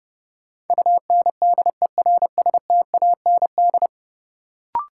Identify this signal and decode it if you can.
Morse code, heard as a keyed tone, about 30 wpm, 700 Hz. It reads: UNDERSTAND
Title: Sets of 4 Words - New Word Drill - 30wpm